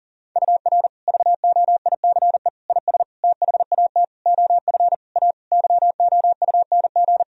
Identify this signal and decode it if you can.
Morse code, heard as a keyed tone, about 40 wpm, 700 Hz. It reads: UR VOICE IS THAT OF A YOUNG
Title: Sentences from Top 500 Words and Abbreviations 40wpm